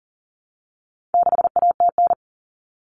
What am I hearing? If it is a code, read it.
Morse code, heard as a keyed tone, about 40 wpm, 700 Hz. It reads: 6ATN
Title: All Cut Numbers - 4-Digit 40wpm